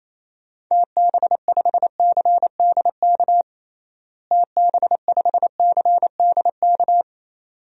Morse code, heard as a keyed tone, about 28 wpm, 700 Hz. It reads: TB5CDK TB5CDK
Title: DXCC-100 Call Signs - 6 Characters - Repeated 2x - 28wpm